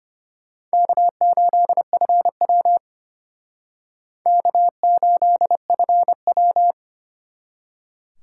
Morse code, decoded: K8FW K8FW

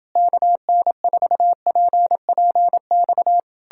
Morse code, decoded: KN4PPX

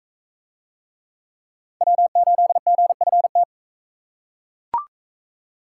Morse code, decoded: W8GFT

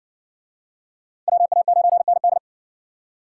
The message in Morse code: WA9ND

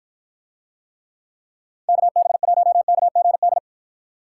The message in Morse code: KB1KGD

